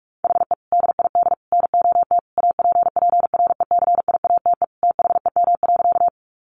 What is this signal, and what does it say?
Morse code, heard as a keyed tone, about 44 wpm, 700 Hz. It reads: HE DID NOT APPRECIATE THEM.
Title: Sentences from Top 1300 Words 45wpm